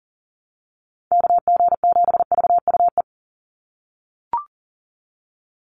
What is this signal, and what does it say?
Morse code, heard as a keyed tone, about 39 wpm, 690 Hz. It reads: KG7VUE
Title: USA Call Signs 40wpm